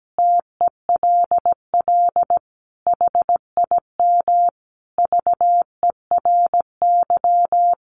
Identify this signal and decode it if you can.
Morse code, heard as a keyed tone, about 17 wpm, 700 Hz. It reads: TELL HIM VERY
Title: Sentences from Top 800 Words 17wpm